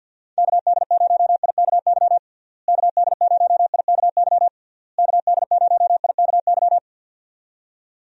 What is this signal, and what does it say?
Morse code, heard as a keyed tone, about 51 wpm, 700 Hz. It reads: KD0IKY KD0IKY KD0IKY
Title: USA Call Signs - 6 Characters - Repeated 3x - 50wpm